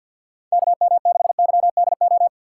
Morse code, heard as a keyed tone, about 51 wpm, 700 Hz. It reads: KM6YDO
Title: USA Call Signs - 6x Character Spacing - 50wpm